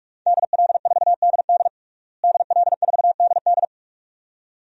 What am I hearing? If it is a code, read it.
Morse code, heard as a keyed tone, about 41 wpm, 700 Hz. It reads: DL4DD DL4DD
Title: DXCC-100 Call Signs - 5 Characters - Repeated 2x - 45wpm